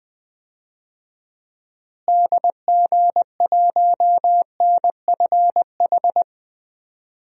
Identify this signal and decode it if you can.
Morse code, heard as a keyed tone, about 20 wpm, 700 Hz. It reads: DG1NFH
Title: DXCC-100 Call Signs - 6 Characters - Repeated 3x - 20wpm